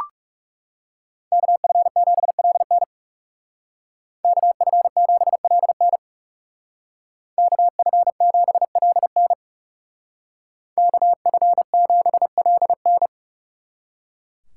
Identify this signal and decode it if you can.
Morse code, heard as a keyed tone, about 36 wpm, 700 Hz. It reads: KF7LN KF7LN KF7LN KF7LN
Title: Speed Racing - USA 2x2 Call Signs 30wpm